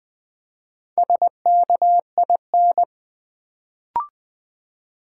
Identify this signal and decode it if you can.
Morse code, heard as a keyed tone, about 20 wpm, 700 Hz. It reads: SKIN
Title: Build A Word - 1 Letter At A Time - 4-Letter Words - 20wpm